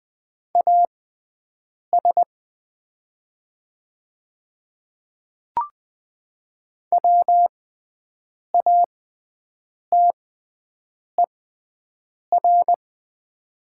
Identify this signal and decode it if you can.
Morse code, heard as a keyed tone, about 20 wpm, spaced as for 7 wpm, 700 Hz. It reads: AS WATER
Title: 6x Character Spacing - Top 100 Words - 20wpm